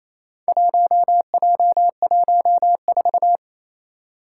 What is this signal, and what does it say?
Morse code, heard as a keyed tone, about 27 wpm, 700 Hz. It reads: 1J14
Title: Four Challenging Letter-Number Combinations 28wpm